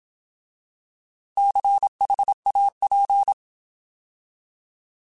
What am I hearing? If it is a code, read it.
Morse code, heard as a keyed tone, about 27 wpm, 800 Hz. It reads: CHAP
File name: Alphabetical